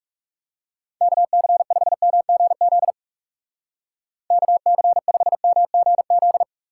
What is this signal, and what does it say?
Morse code, heard as a keyed tone, about 37 wpm, 700 Hz. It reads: KC5MGZ KC5MGZ
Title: Speed Racing - USA 2x3 Call Signs 30wpm